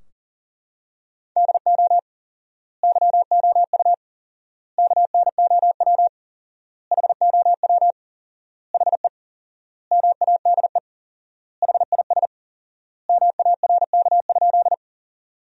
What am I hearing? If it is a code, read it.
Morse code, heard as a keyed tone, about 39 wpm, 700 Hz. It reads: DO YOU KNOW HOW HE MADE HIS MARK?
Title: 4x Word-Spacing - Sentences from Top 300 Words - 40wpm